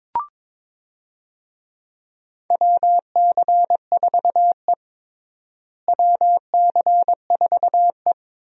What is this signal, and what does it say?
Morse code, heard as a keyed tone, about 22 wpm, 700 Hz. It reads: WC4E WC4E
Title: USA Call Signs - 4 Characters - Repeated 3x - 22wpm